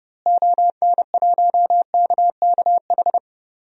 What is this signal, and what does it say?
Morse code, heard as a keyed tone, about 30 wpm, 700 Hz. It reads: ON1KKH
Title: DXCC-100 Call Signs - 6 Characters - Repeated 2x - 30wpm